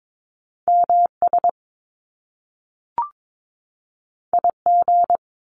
Morse code, heard as a keyed tone, about 22 wpm, 700 Hz. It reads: MS IG
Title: Top Two Letter Combinations in Words 22wpm